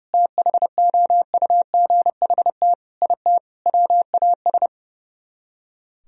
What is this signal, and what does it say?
Morse code, heard as a keyed tone, about 35 wpm, 700 Hz. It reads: THOUGHT IT WAS
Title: Speed-Racing - Sets of 4 Words 30wpm